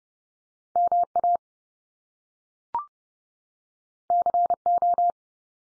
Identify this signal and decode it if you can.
Morse code, heard as a keyed tone, about 30 wpm, 700 Hz. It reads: MA CO
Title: US States 30WPM